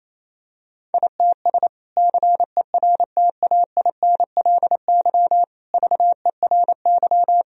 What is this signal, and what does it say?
Morse code, heard as a keyed tone, about 27 wpm, 700 Hz. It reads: ITS CERTAINLY VERY
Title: Sentences from Top 1000 Words - Review Part 2 - 28wpm